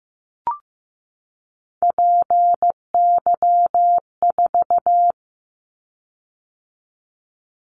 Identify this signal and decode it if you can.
Morse code, heard as a keyed tone, about 15 wpm, 700 Hz. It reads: PY4
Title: Callsign Prefixes for Top 50 DXCC - 15wpm